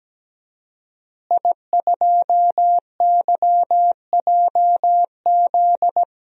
Morse code, heard as a keyed tone, about 17 wpm, 700 Hz. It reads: I2YJZ
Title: DXCC-100 Call Signs - 5 Characters - Repeated 3x - 17wpm